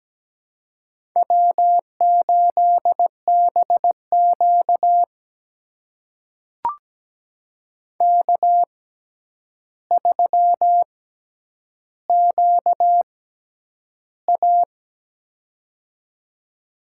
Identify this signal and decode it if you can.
Morse code, heard as a keyed tone, about 17 wpm, spaced as for 9 wpm, 700 Hz. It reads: W8BQ K3QA
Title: USA Call Signs - 4 Characters - 6x Character Spacing - 17wpm